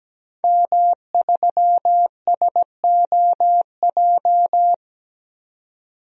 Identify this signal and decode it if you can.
Morse code, heard as a keyed tone, about 17 wpm, 700 Hz. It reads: M3SOJ
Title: DXCC-100 Call Signs - 5 Characters - 17wpm